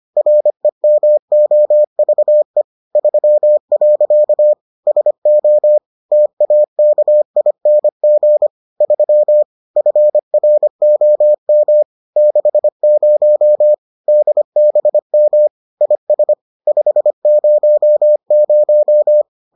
Morse code, heard as a keyed tone, about 25 wpm, 600 Hz. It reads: REMOVE 3. SO TAKING 3 FROM 60 DBM IS 500